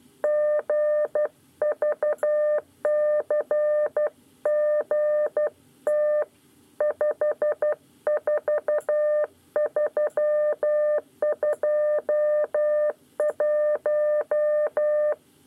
Morse code, heard as a keyed tone, about 12 wpm, 600 Hz. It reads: GVCGT 54321
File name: 9 lezione cw 2